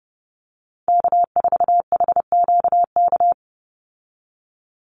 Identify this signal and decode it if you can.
Morse code, heard as a keyed tone, about 30 wpm, 700 Hz. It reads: K4HQK
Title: Call Signs Repeated 2x - 30wpm